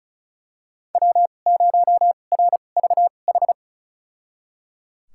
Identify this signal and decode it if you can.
Morse code, heard as a keyed tone, about 36 wpm, spaced as for 25 wpm, 700 Hz. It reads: W0RVH
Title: USA Call Signs - 5 Characters - 2x Character Spacing - 35wpm